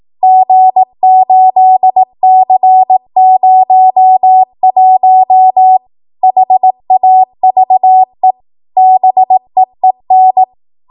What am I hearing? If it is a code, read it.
Morse code, heard as a keyed tone, about 18 wpm, 750 Hz. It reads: G8C01 HAVE BEEN